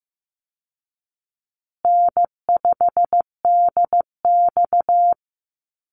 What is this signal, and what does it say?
Morse code, heard as a keyed tone, about 15 wpm, 700 Hz. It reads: N5DX
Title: Easy Call Signs Repeated 3x - 15wpm